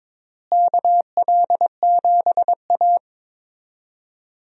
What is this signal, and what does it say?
Morse code, heard as a keyed tone, about 22 wpm, 700 Hz. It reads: KL7A
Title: DXCC-100 Call Signs - 4 Characters - Repeated 3x - 22wpm